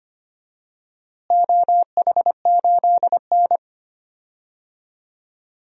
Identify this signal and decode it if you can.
Morse code, heard as a keyed tone, about 25 wpm, 700 Hz. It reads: OH8N
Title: DXCC-100 Call Signs - 4 Characters - Repeated 3x - 25wpm